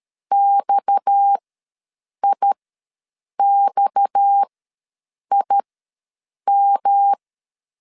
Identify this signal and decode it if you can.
Morse code, heard as a keyed tone, about 13 wpm, spaced as for 7 wpm, 800 Hz. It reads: XIXIM
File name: LEZIONI / 6 PKX0 / 6 PKX ws 12wpm 3106